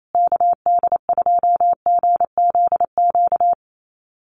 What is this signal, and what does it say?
Morse code, heard as a keyed tone, about 28 wpm, 700 Hz. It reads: KD2GZQ
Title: USA Call Signs Repeated 3x 28wpm